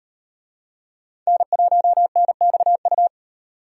Morse code, heard as a keyed tone, about 35 wpm, 700 Hz. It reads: N1NXU